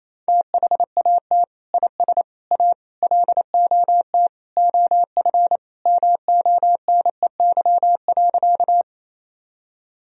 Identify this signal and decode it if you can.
Morse code, heard as a keyed tone, about 28 wpm, 700 Hz. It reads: THAT IS A LOT OF MONEY.
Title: Sentences from Top 1000 Words - Review Part 3 - 28wpm